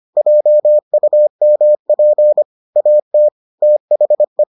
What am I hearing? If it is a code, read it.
Morse code, heard as a keyed tone, about 25 wpm, 600 Hz. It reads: JUMP AT THE